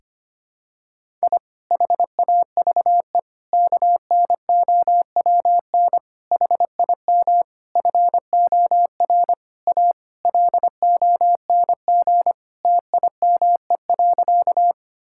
Morse code, heard as a keyed tone, about 25 wpm, 700 Hz. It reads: I HAVE KNOWN HIM FOR A LONG TIME.
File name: practice / 25wpm / Sentences with 4-Word Phrases - 25wpm